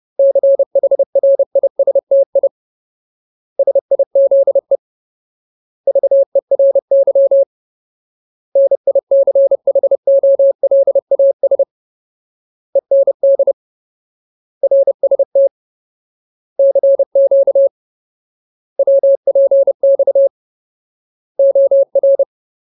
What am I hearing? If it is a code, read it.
Morse code, heard as a keyed tone, about 30 wpm, 550 Hz. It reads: CHRISTI SIZE VERY NICHOLAS END RST CQ WPX OR
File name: Ultimate Word List / 30WPM Ultimate